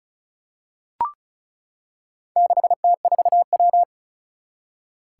Morse code, heard as a keyed tone, about 36 wpm, 700 Hz. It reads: 6T4W
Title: Four Challenging Letter-Number Combinations 35wpm